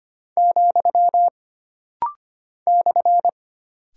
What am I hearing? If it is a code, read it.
Morse code, heard as a keyed tone, about 25 wpm, 700 Hz. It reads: , /
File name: Single Punctuation-Prosign - Rapid-Fire - 25wpm